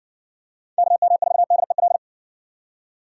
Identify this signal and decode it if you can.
Morse code, heard as a keyed tone, about 60 wpm, 700 Hz. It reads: KM4DEL